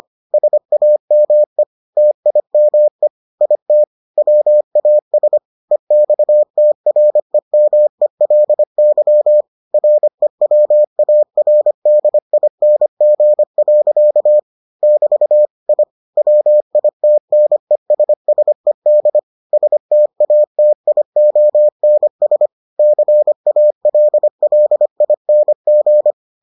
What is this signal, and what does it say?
Morse code, heard as a keyed tone, about 25 wpm, 600 Hz. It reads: SAME TIME IT WAS EXTREMELY REWARDING. <BT> I WITNESSED STATIONS CALLING